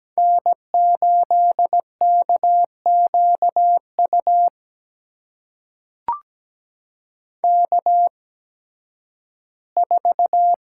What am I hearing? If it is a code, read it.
Morse code, heard as a keyed tone, about 17 wpm, spaced as for 11 wpm, 700 Hz. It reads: N8KQU K4
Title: USA Call Signs - 5 Characters - 8x Character Spacing - 17wpm